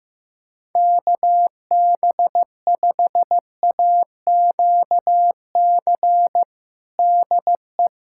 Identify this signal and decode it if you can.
Morse code, heard as a keyed tone, about 15 wpm, 700 Hz. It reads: KB5AQC DE
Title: USA Callsign DE Callsign - 6 Characters - 15wpm